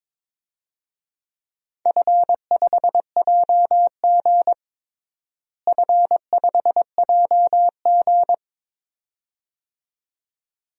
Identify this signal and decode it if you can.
Morse code, heard as a keyed tone, about 23 wpm, 700 Hz. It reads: F5JG F5JG
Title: DXCC-100 Call Signs - 4 Characters - Repeated 2x - 22wpm